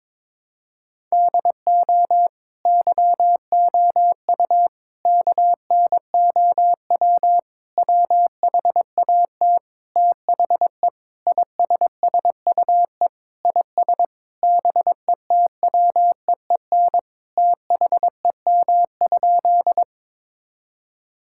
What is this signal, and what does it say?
Morse code, heard as a keyed tone, about 21 wpm, 700 Hz. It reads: DO YOU KNOW WHAT THE ISSUE IS BETWEEN THEM?
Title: Sentences from Top 1000 Words - Review Part 6 - 22wpm